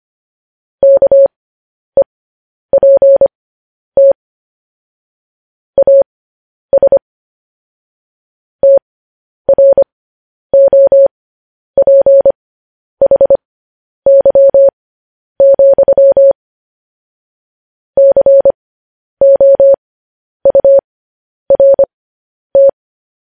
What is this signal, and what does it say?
Morse code, heard as a keyed tone, about 25 wpm, spaced as for 10 wpm, 550 Hz. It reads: KEPT AS TROPHY, COURT
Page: news headlines in morse code at 10 wpm